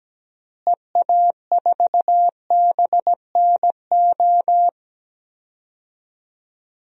Morse code, heard as a keyed tone, about 17 wpm, 700 Hz. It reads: EA4BNO
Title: DXCC-100 Call Signs - 6 Characters - Repeated 2x - 17wpm